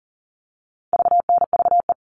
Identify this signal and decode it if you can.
Morse code, heard as a keyed tone, about 40 wpm, 700 Hz. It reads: VNVE
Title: All Cut Numbers - 4-Digit 40wpm